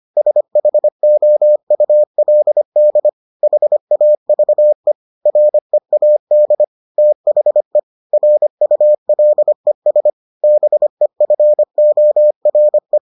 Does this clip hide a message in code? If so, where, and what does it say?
Yes: Morse code from 0.2 to 13.0 s, heard as a keyed tone, about 25 wpm, 600 Hz. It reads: SHOULD HAVE READ THE RULES BEFORE